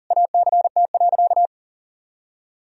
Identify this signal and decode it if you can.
Morse code, heard as a keyed tone, about 40 wpm, 700 Hz. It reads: ACT.
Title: Sentences from Top 1000 Words - Review Part 1 - 40wpm